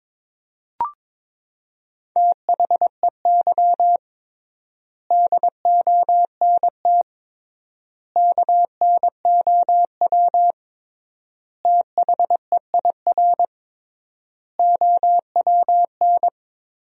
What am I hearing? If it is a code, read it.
Morse code, heard as a keyed tone, about 22 wpm, 700 Hz. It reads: THEY DONT KNOW THEIR OWN
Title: 3x Word-Spacing - Sentences from Top 300 Words - 22wpm